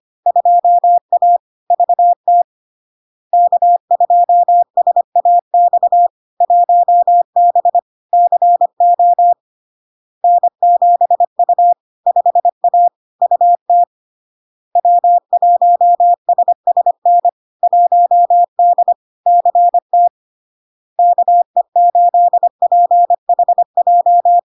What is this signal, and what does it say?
Morse code, heard as a keyed tone, about 25 wpm, 700 Hz. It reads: 2A VT K2SAX 1B CO N7U 5A UT W1SSN 1D CT KE8PHJ
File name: LARK-FD-2022-Log-in-CW-25wpm-700-Hz